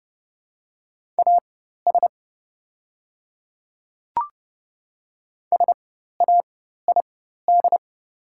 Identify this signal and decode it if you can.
Morse code, heard as a keyed tone, about 30 wpm, spaced as for 14 wpm, 690 Hz. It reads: AS SAID
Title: 4x Character Spacing - Top 300 Words - 30wpm